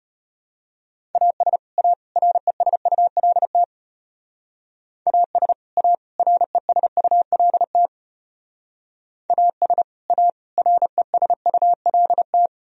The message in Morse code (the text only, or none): AS A RESULT AS A RESULT AS A RESULT